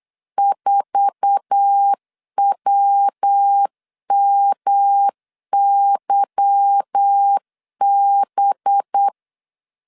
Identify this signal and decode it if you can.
Morse code, heard as a keyed tone, about 8 wpm, 800 Hz. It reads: 4WMYB
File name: 8wpm